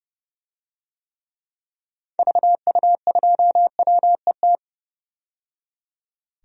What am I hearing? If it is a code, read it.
Morse code, heard as a keyed tone, about 30 wpm, 700 Hz. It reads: VU2WET
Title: DXCC-100 Call Signs - 6 Characters - 30wpm